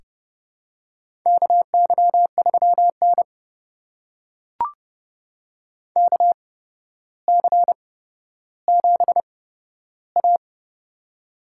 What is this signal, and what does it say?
Morse code, heard as a keyed tone, about 30 wpm, spaced as for 13 wpm, 700 Hz. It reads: KY3N KC7A
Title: USA Call Signs - 4 Characters - 8x Character Spacing - 30wpm